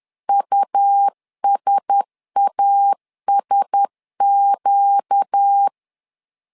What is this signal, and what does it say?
Morse code, heard as a keyed tone, about 10 wpm, 800 Hz. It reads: USASQ
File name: LEZIONI / 4 Q1S 10wpm 2133